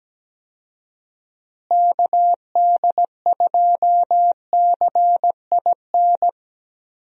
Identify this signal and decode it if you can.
Morse code, heard as a keyed tone, about 17 wpm, 700 Hz. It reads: KD2CIN
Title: USA Call Signs - 6 Characters - Repeated 3x - 17wpm